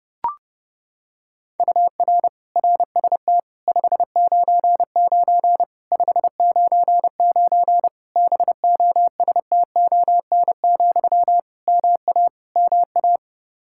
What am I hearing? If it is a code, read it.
Morse code, heard as a keyed tone, about 30 wpm, 700 Hz. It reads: UR RST 599 599 BOSTON, MA MA
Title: Practice QSO - Exchange 1 - Beginning - 30wpm